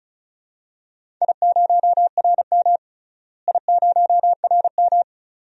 Morse code, heard as a keyed tone, about 36 wpm, 690 Hz. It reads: I0RM I0RM
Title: DXCC-100 Call Signs - 4 Characters - Repeated 2x - 35wpm